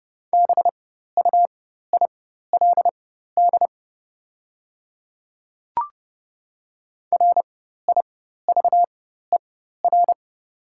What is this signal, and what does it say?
Morse code, heard as a keyed tone, about 31 wpm, spaced as for 14 wpm, 700 Hz. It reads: BUILD RIVER
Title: 4x Character Spacing - Top 300 Words - 30wpm